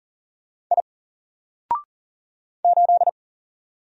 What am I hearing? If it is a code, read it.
Morse code, heard as a keyed tone, about 40 wpm, 700 Hz. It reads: I 8
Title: Single Letter-Number Alt 40wpm